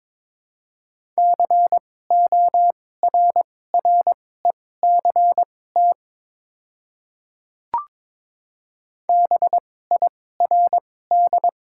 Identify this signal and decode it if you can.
Morse code, heard as a keyed tone, about 22 wpm, spaced as for 16 wpm, 700 Hz. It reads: CORRECT BIRD
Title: 2x Character Spacing - Top 500 Words - 22wpm